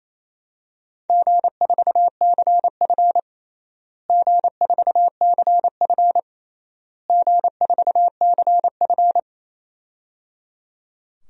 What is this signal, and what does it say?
Morse code, heard as a keyed tone, about 28 wpm, 700 Hz. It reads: G4CF G4CF G4CF
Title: DXCC-100 Call Signs - 4 Characters - Repeated 3x - 28wpm